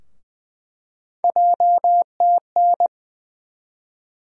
Morse code, heard as a keyed tone, about 20 wpm, 700 Hz. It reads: JTN
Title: USA Call Sign Postfixes - 3 Letters 20wpm